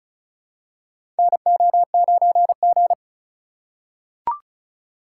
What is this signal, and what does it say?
Morse code, heard as a keyed tone, about 36 wpm, 700 Hz. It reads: NO9G
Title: USA Call Signs - 4 Characters - 2x Character Spacing - 35wpm